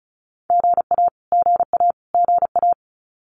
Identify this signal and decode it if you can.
Morse code, heard as a keyed tone, about 34 wpm, 700 Hz. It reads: GA GA GA
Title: Abbreviations Repeated 3x - 35wpm